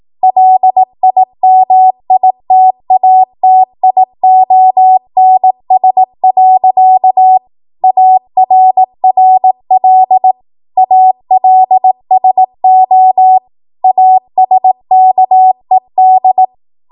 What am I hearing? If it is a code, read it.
Morse code, heard as a keyed tone, about 18 wpm, 750 Hz. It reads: LIMITATIONS. ARRL ALSO ASKED
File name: Morse / 18WPM